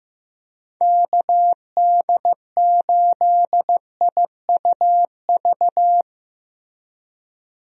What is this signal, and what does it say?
Morse code, heard as a keyed tone, about 15 wpm, 700 Hz. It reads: KD8IUV
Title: USA Call Signs - 6 Characters - Repeated 2x - 15wpm